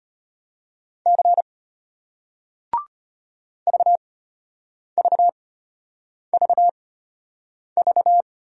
Morse code, heard as a keyed tone, about 34 wpm, spaced as for 8 wpm, 700 Hz. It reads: C VVVV